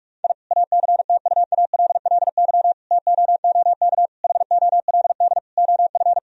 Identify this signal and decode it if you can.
Morse code, heard as a keyed tone, about 41 wpm, 700 Hz. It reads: I ACTUALLY TOOK HOLD OF
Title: Sentences from Top 1000 Words - Review Part 4 - 45wpm